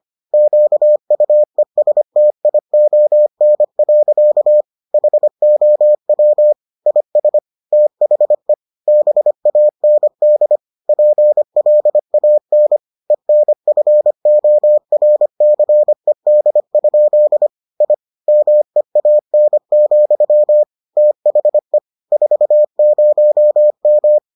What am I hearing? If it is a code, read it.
Morse code, heard as a keyed tone, about 25 wpm, 600 Hz. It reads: QUESTION. HOW IS THE BAND PLAN ENFORCED? I MEAN, THE 40M